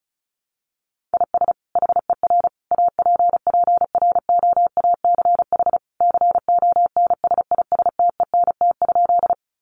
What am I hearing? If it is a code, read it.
Morse code, heard as a keyed tone, about 34 wpm, 700 Hz. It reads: IS HER APPROACH CONSISTENT?
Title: Sentences from Top 1500 Words 35wpm